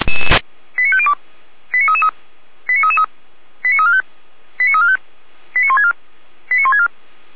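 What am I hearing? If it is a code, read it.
Five-tone selective call (ZVEI), heard as quick runs of five tones, tones of about 70 ms. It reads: 89502 89333 89333 89345 89345 89164 89164
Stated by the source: Five-tone ZVEI selective call received on an SDR.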